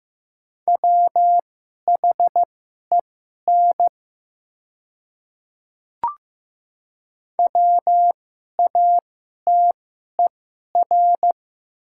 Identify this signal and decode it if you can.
Morse code, heard as a keyed tone, about 15 wpm, spaced as for 11 wpm, 700 Hz. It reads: WHEN WATER
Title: 2x Character Spacing - Top 100 Words - 15wpm